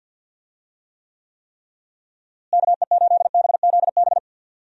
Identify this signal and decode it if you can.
Morse code, heard as a keyed tone, about 49 wpm, 700 Hz. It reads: KE8BZB
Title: USA Call Signs - 6 Characters - Repeated 3x - 50wpm